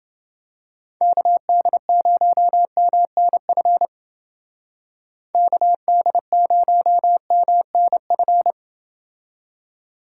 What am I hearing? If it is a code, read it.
Morse code, heard as a keyed tone, about 28 wpm, 700 Hz. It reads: KD0MNF KD0MNF